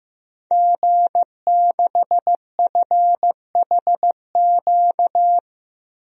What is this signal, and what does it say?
Morse code, heard as a keyed tone, about 15 wpm, 700 Hz. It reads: G6FHQ
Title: DXCC-100 Call Signs - 5 Characters - Repeated 3x - 15wpm